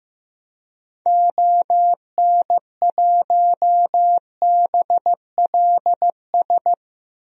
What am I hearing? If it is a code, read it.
Morse code, heard as a keyed tone, about 15 wpm, 700 Hz. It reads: ON1BLS
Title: DXCC-100 Call Signs - 6 Characters - 15wpm